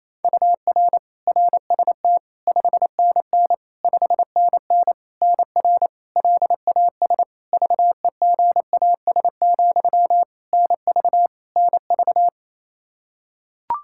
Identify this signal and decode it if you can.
Morse code, heard as a keyed tone, about 27 wpm, 700 Hz. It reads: UR RST 5NN 5NN NR LAS VEGAS, NV NV
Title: K1USN SST QSO Practice 28wpm